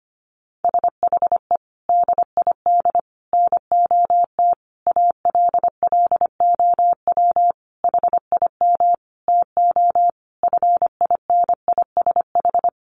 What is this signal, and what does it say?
Morse code, heard as a keyed tone, about 25 wpm, 700 Hz. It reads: SHE DID NOT ALLOW HIM TO FINISH
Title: Sentences from Top 1200 Words 25wpm